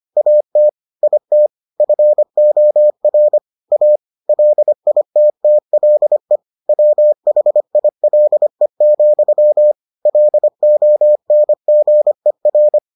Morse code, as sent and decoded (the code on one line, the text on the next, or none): .- - / .. - / ..-. --- .-. / .- / .-.. .. - - .-.. / .-- .... .. .-.. --..-- / .-.. --- -. --. .-.
AT IT FOR A LITTLE WHILE, LONGER